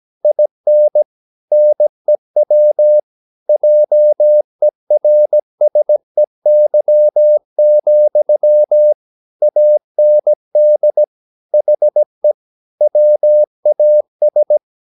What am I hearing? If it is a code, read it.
Morse code, heard as a keyed tone, about 17 wpm, 600 Hz. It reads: IN NEW JERSEY, AND HE WAS
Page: Romanen Never af Ken Follet på engelsk.